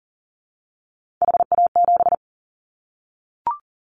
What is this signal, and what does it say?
Morse code, heard as a keyed tone, about 40 wpm, 700 Hz. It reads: HA7
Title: Callsign Prefixes for Top 25 DXCC - 40wpm